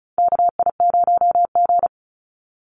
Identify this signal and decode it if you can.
Morse code, heard as a keyed tone, about 34 wpm, 700 Hz. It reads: KI0G
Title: NA SOTA Chaser Callsigns 35wpm